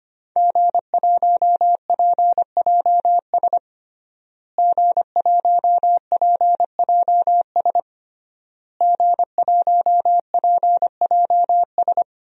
Morse code, heard as a keyed tone, about 25 wpm, 700 Hz. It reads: G1PJS G1PJS G1PJS
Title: DXCC-100 Call Signs - 5 Characters - Repeated 3x - 25wpm